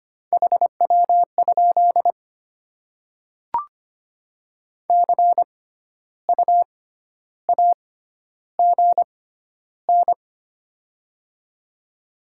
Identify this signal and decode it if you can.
Morse code, heard as a keyed tone, about 25 wpm, spaced as for 12 wpm, 700 Hz. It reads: HW? CU AGN
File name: Top 20 QSO Elements - 6x Character Spacing - 25wpm